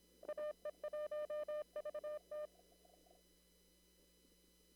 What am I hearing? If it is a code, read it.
Morse code, heard as a keyed tone, about 26 wpm, 600 Hz. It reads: AE1VT
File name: CW / kt5j wpx cw-op-k5pi